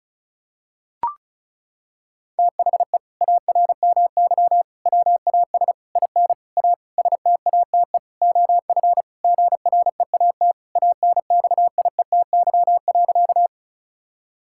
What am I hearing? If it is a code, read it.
Morse code, heard as a keyed tone, about 36 wpm, 700 Hz. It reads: THE ARMY WAS IN A STATE OF GREAT ANXIETY.
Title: Sentences from Top 1600 Words 35wpm